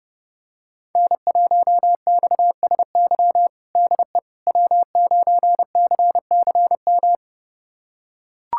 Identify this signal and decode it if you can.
Morse code, heard as a keyed tone, about 30 wpm, 700 Hz. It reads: N1XSY DE W9CCM
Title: USA Callsign DE Callsign - 5 Characters - 30wpm